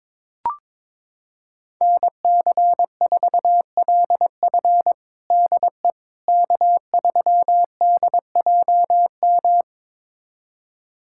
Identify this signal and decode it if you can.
Morse code, heard as a keyed tone, about 22 wpm, 700 Hz. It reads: NC4LF DE K3DJM
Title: USA Callsign DE Callsign - 5 Characters - 22wpm